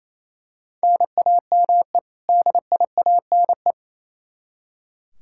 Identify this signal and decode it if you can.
Morse code, heard as a keyed tone, about 27 wpm, 700 Hz. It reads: NAME DIANE
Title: K1USN SST QSO Practice 28wpm